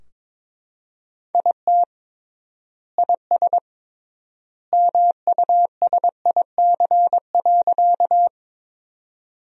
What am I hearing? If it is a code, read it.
Morse code, heard as a keyed tone, about 22 wpm, 700 Hz. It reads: IT IS MUSIC.
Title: 3x Word-Spacing - Sentences from Top 300 Words - 22wpm